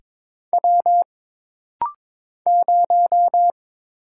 Morse code, heard as a keyed tone, about 22 wpm, 700 Hz. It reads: W 0
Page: This rapid-fire timing is geared to push your ICR (Instant Character Recognition) skills.